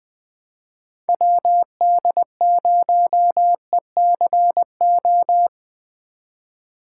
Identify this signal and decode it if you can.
Morse code, heard as a keyed tone, about 20 wpm, 700 Hz. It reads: WD0ECO
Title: K1USN SST QSO Practice - 20wpm